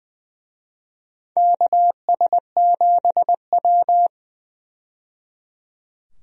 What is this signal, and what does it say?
Morse code, heard as a keyed tone, about 20 wpm, 700 Hz. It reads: KS7W
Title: USA Call Signs - 4 Characters - 20wpm